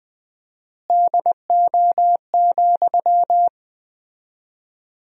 Morse code, heard as a keyed tone, about 20 wpm, 700 Hz. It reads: DO,
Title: 4x Word-Spacing - Sentences from Top 100 Words - 20wpm